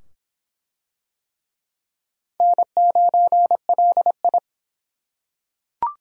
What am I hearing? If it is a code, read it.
Morse code, heard as a keyed tone, about 26 wpm, 700 Hz. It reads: N9LI